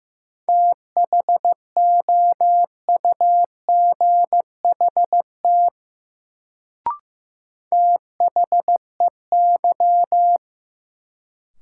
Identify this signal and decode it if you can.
Morse code, heard as a keyed tone, about 15 wpm, 700 Hz. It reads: THOUGHT THEY
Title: Sets of 2 Words - New Word Drill - 15wpm